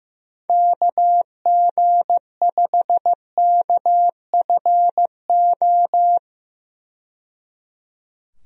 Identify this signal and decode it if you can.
Morse code, heard as a keyed tone, about 15 wpm, 700 Hz. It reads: KG5KFO
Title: USA Call Signs - 6 Characters - 15wpm